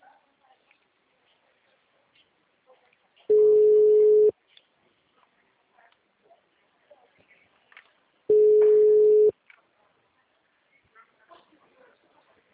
Ringback tone (Europe, 425 Hz).